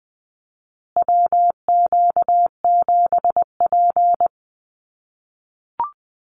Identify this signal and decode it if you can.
Morse code, heard as a keyed tone, about 20 wpm, 700 Hz. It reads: WQ7P
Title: USA Call Signs Repeated 2x 20wpm